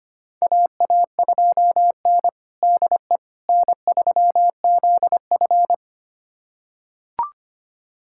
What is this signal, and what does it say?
Morse code, heard as a keyed tone, about 25 wpm, 700 Hz. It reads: AA2N DE N3ZF
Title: USA Callsign DE Callsign - 4 Characters - 25wpm